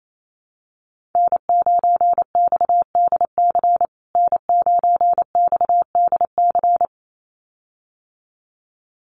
Morse code, heard as a keyed tone, about 28 wpm, 700 Hz. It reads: N9XDC N9XDC
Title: USA Call Signs Repeated 2x 28wpm